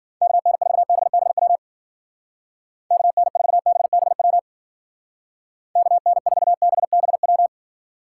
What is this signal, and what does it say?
Morse code, heard as a keyed tone, about 50 wpm, 700 Hz. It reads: KN4BBW KN4BBW KN4BBW
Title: Speed Racing - USA 2x3 Call Signs 40wpm